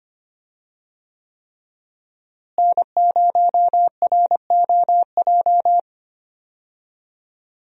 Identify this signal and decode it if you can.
Morse code, heard as a keyed tone, about 25 wpm, 700 Hz. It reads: N0ROJ
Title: USA Call Signs - 5 Characters - Repeated 3x - 25wpm